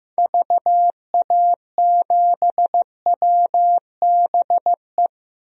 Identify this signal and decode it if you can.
Morse code, heard as a keyed tone, about 15 wpm, 700 Hz. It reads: VA7WBE
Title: DXCC-100 Call Signs - 6 Characters - 15wpm